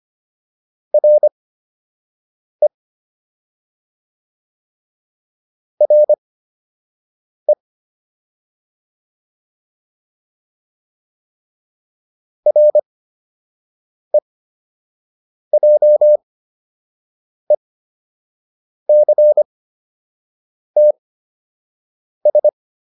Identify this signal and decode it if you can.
Morse code, heard as a keyed tone, about 25 wpm, spaced as for 6 wpm, 600 Hz. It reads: RE RE REJECTS